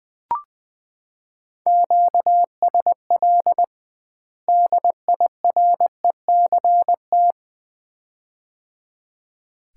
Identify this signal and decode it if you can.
Morse code, heard as a keyed tone, about 19 wpm, 700 Hz. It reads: QSL DIRECT
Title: Common QSO Phrases - 2x Word-Spacing - 20wpm